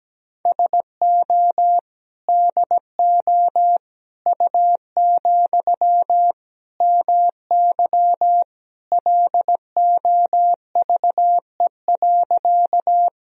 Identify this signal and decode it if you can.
Morse code, heard as a keyed tone, about 17 wpm, 700 Hz. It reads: SO DO U, MY LOVE.
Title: Sentences from Top 400 Words and Abbreviations 17wpm